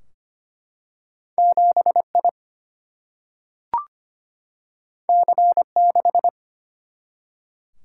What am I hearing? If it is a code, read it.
Morse code, heard as a keyed tone, about 25 wpm, 700 Hz. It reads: 7I C6
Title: Single-Letter Single-Number 25wpm